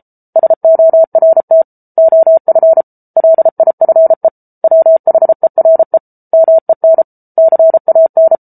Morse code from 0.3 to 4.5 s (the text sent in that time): SORT OF LIFE